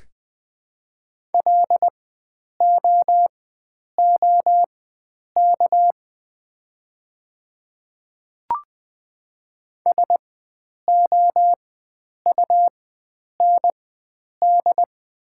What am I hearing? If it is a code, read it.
Morse code, heard as a keyed tone, about 20 wpm, spaced as for 9 wpm, 700 Hz. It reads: LOOK SOUND
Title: 4x Character Spacing - Top 100 Words - 20wpm